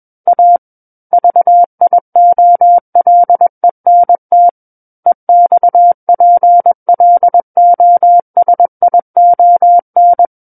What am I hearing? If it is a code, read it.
Morse code, heard as a keyed tone, about 21 wpm, 700 Hz. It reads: A VIOLENT EXPLOSION
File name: Book One / 12-Chapter 12 - 21 WPM